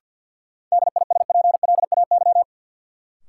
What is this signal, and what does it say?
Morse code, heard as a keyed tone, about 51 wpm, 700 Hz. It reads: DISPLAY
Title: Sentences from Top 1200 Words - New Word Drill Alt - 50wpm